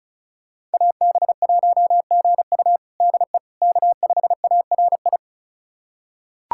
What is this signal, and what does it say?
Morse code, heard as a keyed tone, about 36 wpm, 700 Hz. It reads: AB1GU DE K5ARI
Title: USA Callsign DE Callsign - 5 Characters - 35wpm